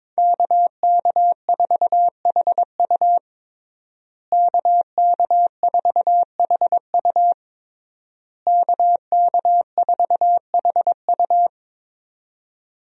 Morse code, heard as a keyed tone, about 21 wpm, 700 Hz. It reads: KK4HU KK4HU KK4HU
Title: USA Call Signs - 5 Characters - Repeated 3x - 22wpm